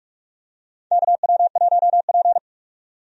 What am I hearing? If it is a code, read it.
Morse code, heard as a keyed tone, about 46 wpm, 700 Hz. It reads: KW1P
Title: USA Call Signs - 4 Characters - 45wpm